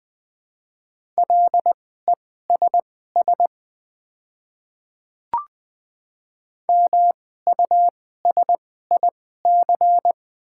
Morse code, heard as a keyed tone, about 20 wpm, spaced as for 15 wpm, 700 Hz. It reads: LESS MUSIC
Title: 2x Character Spacing - Top 500 Words - 20wpm